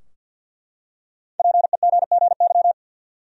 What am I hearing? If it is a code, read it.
Morse code, heard as a keyed tone, about 50 wpm, 700 Hz. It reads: PEGGY
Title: USA Common Names - Max 5 Letters - 50wpm